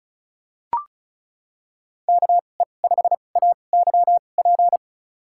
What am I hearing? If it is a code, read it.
Morse code, heard as a keyed tone, about 34 wpm, spaced as for 25 wpm, 700 Hz. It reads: KE5AYP
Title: USA Call Signs - 6 Characters - 2x Character Spacing - 35wpm